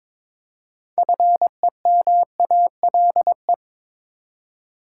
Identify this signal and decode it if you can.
Morse code, heard as a keyed tone, about 22 wpm, 700 Hz. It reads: FEMALE
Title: Six Letter Words - 22wpm